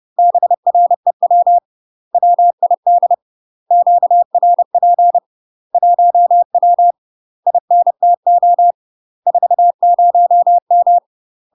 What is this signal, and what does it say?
Morse code, heard as a keyed tone, about 30 wpm, 700 Hz. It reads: BREW WID QRP 1W INTO 40M
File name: UK MASTER QSOs TRAINING/30WPM